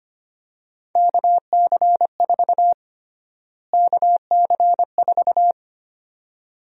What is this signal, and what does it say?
Morse code, heard as a keyed tone, about 25 wpm, 700 Hz. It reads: KC4 KC4
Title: USA Call Sign Prefixes - Repeated 2x - 25wpm